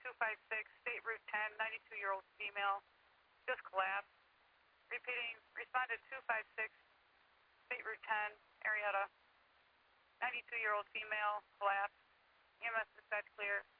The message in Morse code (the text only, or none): none